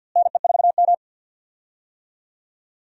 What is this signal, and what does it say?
Morse code, heard as a keyed tone, about 49 wpm, 700 Hz. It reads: NE4M
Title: USA Call Signs - 4 Characters - Repeated 3x - 50wpm